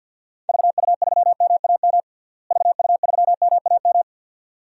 Morse code, heard as a keyed tone, about 46 wpm, 700 Hz. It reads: VU3MAM VU3MAM
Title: DXCC-100 Call Signs - 6 Characters - Repeated 2x - 50wpm